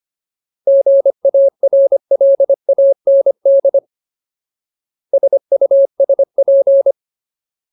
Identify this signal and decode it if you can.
Morse code, heard as a keyed tone, about 25 wpm, 550 Hz. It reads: GARLAND SUSP
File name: Ultimate Word List / 25WPM Ultimate